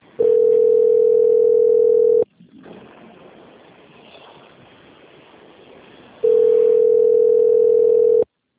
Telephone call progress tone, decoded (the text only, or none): ringback tone